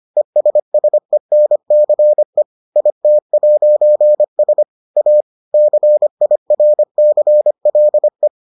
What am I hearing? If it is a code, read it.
Morse code, heard as a keyed tone, about 25 wpm, 600 Hz. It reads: ESSENCE IT'S A CIRCLE